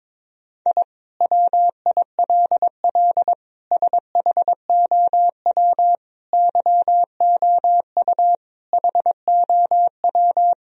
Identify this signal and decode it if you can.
Morse code, heard as a keyed tone, about 22 wpm, 700 Hz. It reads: I WILL SHOW YOU HOW
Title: Sentences from Top 1000 Words - Review Part 1 - 22wpm